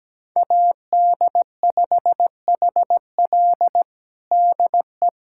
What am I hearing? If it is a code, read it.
Morse code, heard as a keyed tone, about 17 wpm, 700 Hz. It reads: AD5HL DE
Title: USA Callsign DE Callsign - 5 Characters - 17wpm